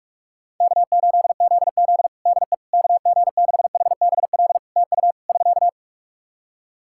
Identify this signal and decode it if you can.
Morse code, heard as a keyed tone, about 46 wpm, 700 Hz. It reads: K8ZZ DE KG6HBL TU <SK>
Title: Practice QSO - Exchange 4 - Ending - 45wpm